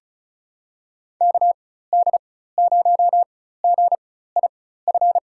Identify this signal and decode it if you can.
Morse code, heard as a keyed tone, about 36 wpm, spaced as for 16 wpm, 700 Hz. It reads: KD0GIF
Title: USA Call Signs - 6 Characters - 4x Character Spacing - 35wpm